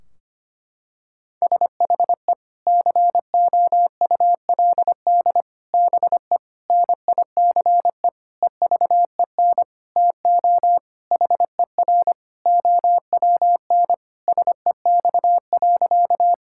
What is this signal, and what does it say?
Morse code, heard as a keyed tone, about 25 wpm, 700 Hz. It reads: SHE COULD BE NICE EVEN TO HER OWN SEX.
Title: Sentences from Top 1600 Words 25wpm